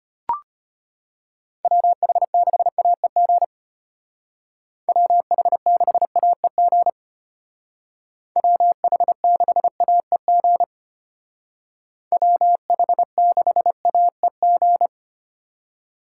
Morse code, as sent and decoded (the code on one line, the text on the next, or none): .-- .... -.... .- . --. / .-- .... -.... .- . --. / .-- .... -.... .- . --. / .-- .... -.... .- . --.
WH6AEG WH6AEG WH6AEG WH6AEG